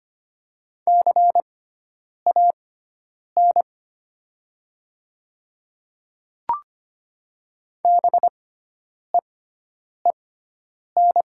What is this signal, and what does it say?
Morse code, heard as a keyed tone, about 25 wpm, spaced as for 9 wpm, 700 Hz. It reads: CAN BEEN
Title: 6x Character Spacing - Top 100 Words - 25wpm